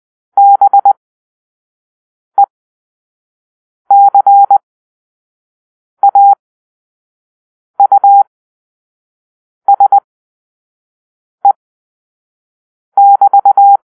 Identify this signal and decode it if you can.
Morse code, heard as a keyed tone, about 20 wpm, spaced as for 5 wpm, 800 Hz. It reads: BECAUSE <BT>
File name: arrl / Grp-4 Drill-1